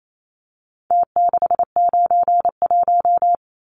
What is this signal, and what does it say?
Morse code, heard as a keyed tone, about 28 wpm, 700 Hz. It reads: T691
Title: Numbers - Contest Style - 4-Digit 28wpm